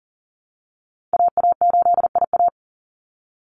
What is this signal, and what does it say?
Morse code, heard as a keyed tone, about 40 wpm, 700 Hz. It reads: AA8IA
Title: Call Signs Repeated 2x - 40wpm